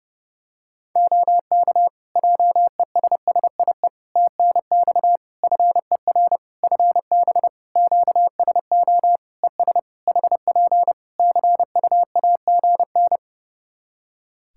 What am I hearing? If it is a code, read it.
Morse code, heard as a keyed tone, about 30 wpm, 700 Hz. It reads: OK JESSIE TNX FER FB QSO ES HP CUAGN
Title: Practice QSO - Exchange 4 - Ending - 30wpm